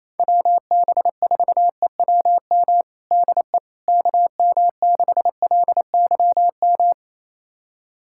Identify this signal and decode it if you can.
Morse code, heard as a keyed tone, about 28 wpm, 700 Hz. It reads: WB4EWM DE KM6LYM
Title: USA Callsign DE Callsign 28wpm